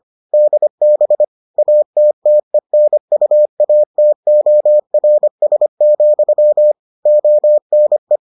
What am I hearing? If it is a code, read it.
Morse code, heard as a keyed tone, about 25 wpm, 600 Hz. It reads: DB ATTENUATORS, ONE